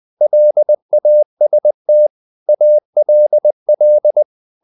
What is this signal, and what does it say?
Morse code, heard as a keyed tone, about 20 wpm, 600 Hz. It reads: LAST ALL
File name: the old man and the sea part 2 20wpm